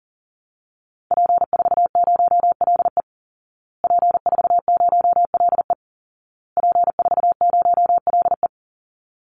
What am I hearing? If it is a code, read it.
Morse code, heard as a keyed tone, about 40 wpm, 700 Hz. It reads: P40LE P40LE P40LE
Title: Call Signs Repeated 2x - 40wpm